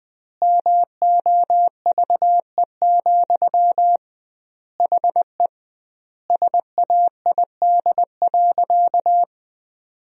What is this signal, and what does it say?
Morse code, heard as a keyed tone, about 20 wpm, 700 Hz. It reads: MOVE, HE SAID.
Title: 2x Word-Spacing - Sentences from Top 300 Words - 20wpm